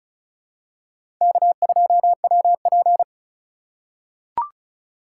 Morse code, heard as a keyed tone, about 34 wpm, 700 Hz. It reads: K2WP
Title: USA Call Signs - 4 Characters - 2x Character Spacing - 35wpm